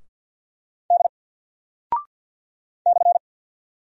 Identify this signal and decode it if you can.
Morse code, heard as a keyed tone, about 48 wpm, 700 Hz. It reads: D /
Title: Single Character (Letter:Number:Punctuation:Prosign) 50wpm